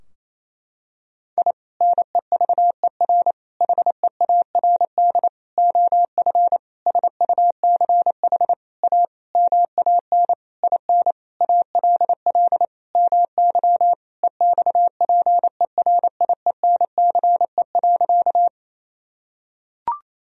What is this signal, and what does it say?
Morse code, heard as a keyed tone, about 28 wpm, 700 Hz. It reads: I NEVER HEARD OF SUCH A MAN IN ALL MY EXPERIENCE.
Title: Sentences from Top 1000 Words - Review Part 3 - 28wpm